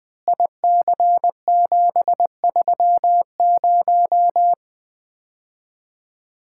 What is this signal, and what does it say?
Morse code, heard as a keyed tone, about 21 wpm, 700 Hz. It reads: IC730
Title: Common QSO Phrases - 4x Word-Spacing - 20wpm